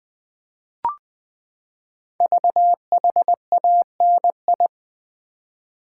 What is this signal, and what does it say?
Morse code, heard as a keyed tone, about 20 wpm, 700 Hz. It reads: VHANI
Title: Code Groups - 5 Letters - 20wpm